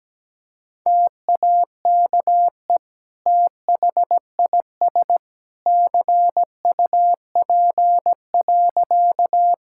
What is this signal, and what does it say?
Morse code, heard as a keyed tone, about 17 wpm, 700 Hz. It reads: TAKE THIS CUP.
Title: Sentences from Top 1600 Words 17wpm